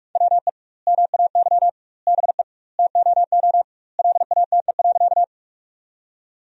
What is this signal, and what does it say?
Morse code, heard as a keyed tone, about 46 wpm, 700 Hz. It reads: WE MAY BE TOO LATE.
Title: 2x Word-Spacing - Sentences from Top 300 Words - 45wpm